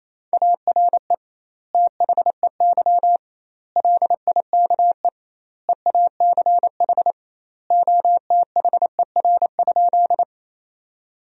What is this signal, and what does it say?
Morse code, heard as a keyed tone, about 28 wpm, 700 Hz. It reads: ARE THEY LIKE EACH OTHER?
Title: 2x Word-Spacing - Sentences from Top 200 Words - 28wpm